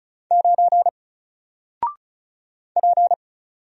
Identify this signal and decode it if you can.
Morse code, heard as a keyed tone, about 35 wpm, 700 Hz. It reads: 9 P
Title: Single Character (Letter:Number:Punctuation:Prosign) 35wpm